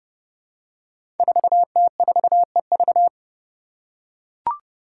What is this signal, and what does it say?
Morse code, heard as a keyed tone, about 29 wpm, 700 Hz. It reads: 4T4EV
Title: Five Challenging Letter-Number Combinations 30wpm